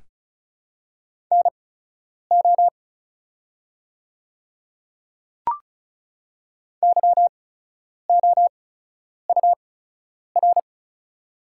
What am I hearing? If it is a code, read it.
Morse code, heard as a keyed tone, about 36 wpm, spaced as for 10 wpm, 700 Hz. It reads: NO YOUR
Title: 8x Character Spacing - Top 100 Words - 35wpm